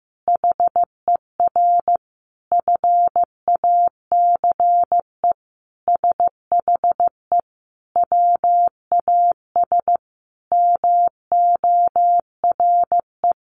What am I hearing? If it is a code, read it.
Morse code, heard as a keyed tone, about 15 wpm, 700 Hz. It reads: HER FACE SHE WAS MORE
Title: Sentences from Top 1200 Words 15wpm